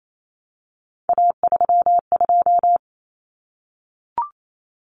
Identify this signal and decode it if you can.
Morse code, heard as a keyed tone, about 28 wpm, 700 Hz. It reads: A32
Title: Common Cut Numbers - 3-Digit 28wpm